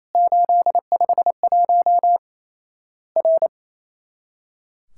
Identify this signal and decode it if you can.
Morse code, heard as a keyed tone, about 27 wpm, 700 Hz. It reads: 851 R
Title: Worked All Europe DX Contest - QTC Exchange - 28wpm